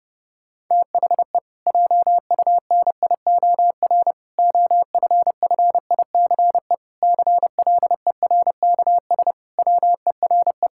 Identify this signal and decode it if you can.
Morse code, heard as a keyed tone, about 30 wpm, 700 Hz. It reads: THE JUNIOR OFFICE CLERKS WERE
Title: Pangram Challenge v2 - 30wpm